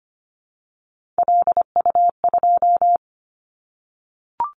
Morse code, heard as a keyed tone, about 25 wpm, 700 Hz. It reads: LU2
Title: Callsign Prefixes for Top 50 DXCC - 25wpm